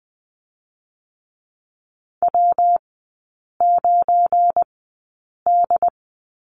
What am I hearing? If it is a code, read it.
Morse code, heard as a keyed tone, about 20 wpm, spaced as for 8 wpm, 700 Hz. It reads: W9D
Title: 191 Lesson 31e - Letter v - Three Characters 20wpm